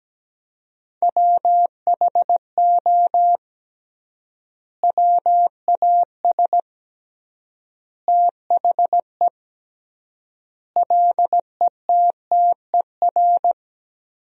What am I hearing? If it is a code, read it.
Morse code, heard as a keyed tone, about 17 wpm, 700 Hz. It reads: WHO WAS THE LETTER
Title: 3x Word-Spacing - Sentences from Top 300 Words - 17wpm